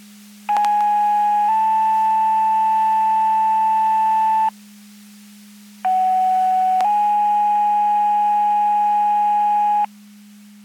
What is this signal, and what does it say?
Two-tone page, heard as a one-second tone, then a three-second tone, about 830 then 880 Hz, about 750 then 830 Hz. It is heard from 0.5 to 9.9 s.
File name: Portage 154.385 Paging